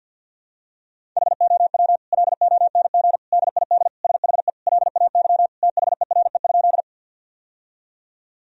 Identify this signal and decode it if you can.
Morse code, heard as a keyed tone, about 50 wpm, 700 Hz. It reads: HOW LONG DID SHE LAY THERE?
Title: Sentences from Top 1000 Words - Review Part 2 - 50wpm